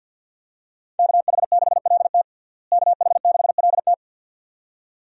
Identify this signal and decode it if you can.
Morse code, heard as a keyed tone, about 53 wpm, 700 Hz. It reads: KH6LT KH6LT
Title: USA Call Signs - 5 Characters - Repeated 2x - 50wpm